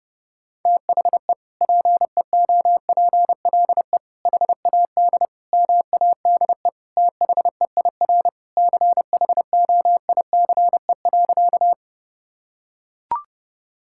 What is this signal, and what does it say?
Morse code, heard as a keyed tone, about 30 wpm, 700 Hz. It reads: THE PEOPLE HAD MADE THEIR CHOICE.
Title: Sentences from Top 1000 Words - Review Part 6 - 30wpm